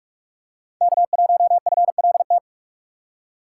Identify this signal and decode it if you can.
Morse code, heard as a keyed tone, about 46 wpm, 700 Hz. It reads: K1FLT
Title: USA Call Signs - 5 Characters - 3x Character Spacing - 45wpm